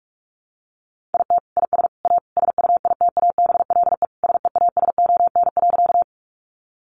Morse code, heard as a keyed tone, about 44 wpm, 700 Hz. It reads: IT IS A SUITABLE SEASON.
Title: Sentences from Top 1400 Words 45wpm